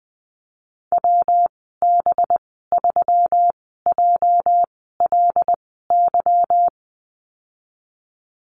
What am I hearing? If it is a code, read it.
Morse code, heard as a keyed tone, about 20 wpm, spaced as for 15 wpm, 700 Hz. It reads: WB3JLY
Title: 245 Lesson 39g - Letter z - Call Signs 20wpm